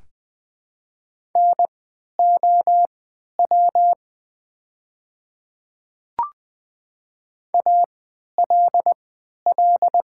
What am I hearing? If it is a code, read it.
Morse code, heard as a keyed tone, about 20 wpm, spaced as for 11 wpm, 700 Hz. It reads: NOW ALL
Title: 3x Character Spacing - Top 500 Words - 20wpm